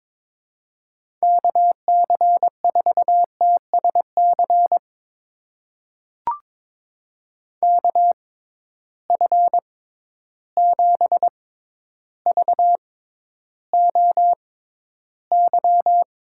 Morse code, heard as a keyed tone, about 21 wpm, spaced as for 11 wpm, 700 Hz. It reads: KC4TSC KF7VOY
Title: USA Call Signs - 6 Characters - 6x Character Spacing - 22wpm